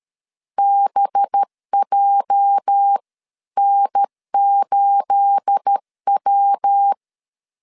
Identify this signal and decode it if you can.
Morse code, heard as a keyed tone, about 13 wpm, spaced as for 11 wpm, 800 Hz. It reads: BJN8W
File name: LEZIONI / 10 SIGNS / 10B INT 12wpm 2414